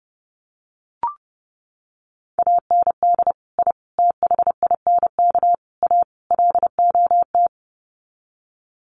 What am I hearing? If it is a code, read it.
Morse code, heard as a keyed tone, about 30 wpm, 700 Hz. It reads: AND I THINK A LOT
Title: Sets of 5 Words Encore2 30wpm